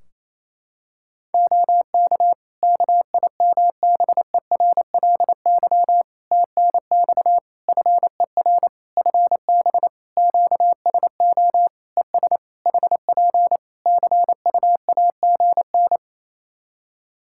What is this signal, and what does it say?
Morse code, heard as a keyed tone, about 28 wpm, 700 Hz. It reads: OK KIMBERLY TNX FER FB QSO ES HP CUAGN
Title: Practice QSO - Exchange 4 - Ending - 28wpm